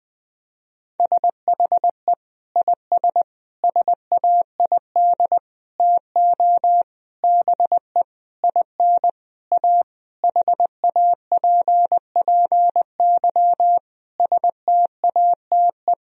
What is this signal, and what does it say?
Morse code, heard as a keyed tone, about 21 wpm, 700 Hz. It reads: SHE IS SAID TO BE IN A HAPPY STATE
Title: Sentences from Top 1000 Words - Review Part 2 - 20wpm